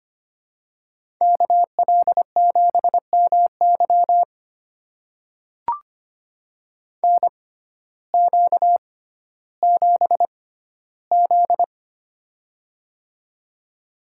Morse code, heard as a keyed tone, about 25 wpm, spaced as for 14 wpm, 700 Hz. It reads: KL7MY NQ7Z
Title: USA Call Signs - 6x Character Spacing - 25wpm